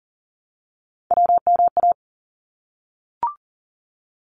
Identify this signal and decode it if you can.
Morse code, heard as a keyed tone, about 40 wpm, 700 Hz. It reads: WMA
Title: ARRL Sections 40wpm